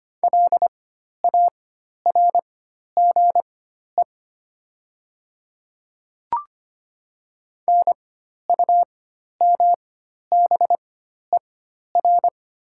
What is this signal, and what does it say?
Morse code, heard as a keyed tone, about 25 wpm, spaced as for 12 wpm, 700 Hz. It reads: LARGE NUMBER
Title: 4x Character Spacing - Top 300 Words - 25wpm